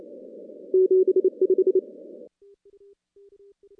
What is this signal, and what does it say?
Morse code, heard as a keyed tone, about 27 wpm, 410 Hz. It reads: 75
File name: contests / CW-SS-k5tr- -2300